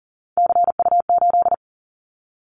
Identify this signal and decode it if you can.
Morse code, heard as a keyed tone, about 40 wpm, 700 Hz. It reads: CU8
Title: Callsign Prefixes for Top 100 DXCC - 40wpm